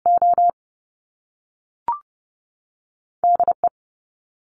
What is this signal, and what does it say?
Morse code, heard as a keyed tone, about 30 wpm, spaced as for 6 wpm, 700 Hz. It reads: ODE
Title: US States 30wpm